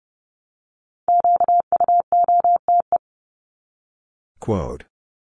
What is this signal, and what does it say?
Morse code, heard as a keyed tone, about 30 wpm, 700 Hz. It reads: QUOTE
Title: Top 1751-2000 Words 30wpm